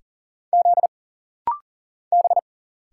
Morse code, heard as a keyed tone, about 39 wpm, 700 Hz. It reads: Z B
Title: Single Letter-Number - Rapid-Fire 40wpm